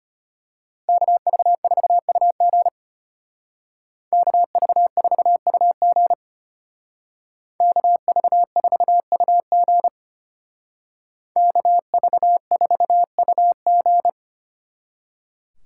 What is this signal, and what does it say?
Morse code, heard as a keyed tone, about 33 wpm, 700 Hz. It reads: KV4UG KV4UG KV4UG KV4UG